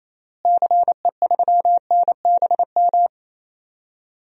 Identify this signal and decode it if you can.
Morse code, heard as a keyed tone, about 28 wpm, 700 Hz. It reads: CE3NBM
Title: DXCC-100 Call Signs - 6 Characters - Repeated 3x - 28wpm